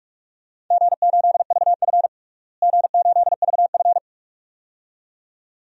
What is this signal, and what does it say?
Morse code, heard as a keyed tone, about 46 wpm, 700 Hz. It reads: G8VF G8VF
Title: DXCC-100 Call Signs - 4 Characters - Repeated 2x - 45wpm